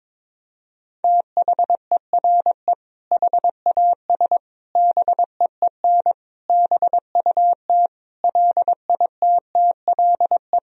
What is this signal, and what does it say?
Morse code, heard as a keyed tone, about 21 wpm, 700 Hz. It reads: THERE HAS BEEN BUT LITTLE
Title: Sentences from Top 1000 Words - Review Part 1 - 22wpm